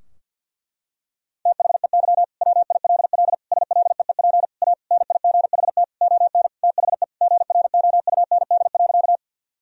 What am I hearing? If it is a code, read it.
Morse code, heard as a keyed tone, about 50 wpm, 700 Hz. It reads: THEY WILL SLEEP A NIGHT ON THE GROUND.
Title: Sentences from Top 1000 Words - Review Part 2 - 50wpm